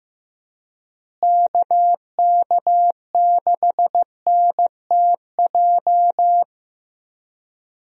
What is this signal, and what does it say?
Morse code, heard as a keyed tone, about 15 wpm, 700 Hz. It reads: KK6NTJ
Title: USA Call Signs - 6 Characters - Repeated 3x - 15wpm